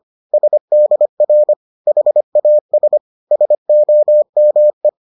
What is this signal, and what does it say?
Morse code, heard as a keyed tone, about 25 wpm, 600 Hz. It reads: SDR HAS SOME